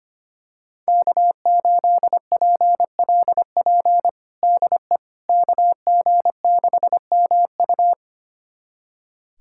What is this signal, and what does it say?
Morse code, heard as a keyed tone, about 25 wpm, 700 Hz. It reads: K8PLP DE KG6MU
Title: USA Callsign DE Callsign - 5 Characters - 25wpm